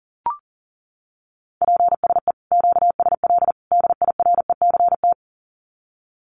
Morse code, heard as a keyed tone, about 40 wpm, 700 Hz. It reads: PSE QSL DIRECT
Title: Common QSO Phrases 40wpm